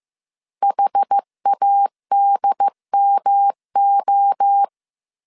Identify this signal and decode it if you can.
Morse code, heard as a keyed tone, about 15 wpm, 800 Hz. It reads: HADMO
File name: CW / DET / 1 3 1500C S A 14wpm 2447